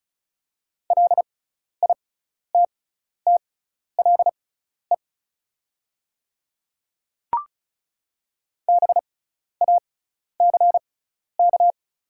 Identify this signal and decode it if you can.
Morse code, heard as a keyed tone, about 36 wpm, spaced as for 12 wpm, 690 Hz. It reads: LITTLE BACK
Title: 6x Character Spacing - Top 300 Words - 35wpm